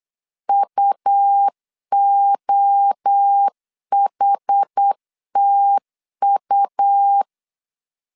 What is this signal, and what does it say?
Morse code, heard as a keyed tone, about 8 wpm, 800 Hz. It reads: UOHTU
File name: CW / DET / 1 3 1500C S B 8wpm 4141